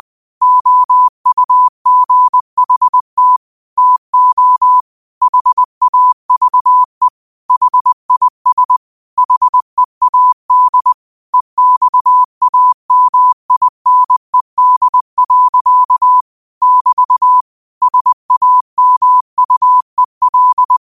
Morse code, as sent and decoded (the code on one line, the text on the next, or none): --- ..- --. .... - / - --- / .... .- ...- . / .... .. ... / .... .- -.. / . -..- .- -- .. -. -.. .-.-.- / -...- / ... .- -- ..- . .-..
OUGHT TO HAVE HIS HEAD EXAMINED. <BT> SAMUEL